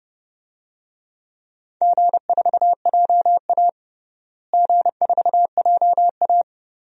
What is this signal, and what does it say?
Morse code, heard as a keyed tone, about 30 wpm, 700 Hz. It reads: G4JA G4JA
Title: DXCC-100 Call Signs - 4 Characters - Repeated 3x - 30wpm